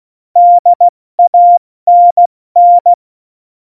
Morse code, heard as a keyed tone, about 16 wpm, 700 Hz. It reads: DANN
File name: Morsekurs / Witze / Witz 03.Dezember 2019, 70 F80